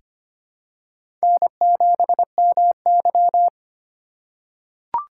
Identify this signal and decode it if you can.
Morse code, heard as a keyed tone, about 25 wpm, 700 Hz. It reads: N7MY
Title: USA Call Signs - 4 Characters - 4x Character Spacing - 25wpm